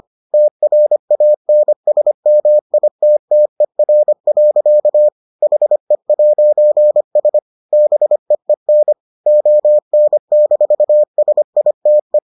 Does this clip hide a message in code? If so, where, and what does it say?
Yes: Morse code from 0.3 to 12.2 s, heard as a keyed tone, about 25 wpm, 600 Hz. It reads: TRANSMITTER. HE'S BEEN ON-SITE